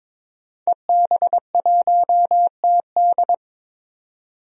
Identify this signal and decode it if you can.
Morse code, heard as a keyed tone, about 22 wpm, 700 Hz. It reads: EB1TD
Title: DXCC-100 Call Signs - 5 Characters - 22wpm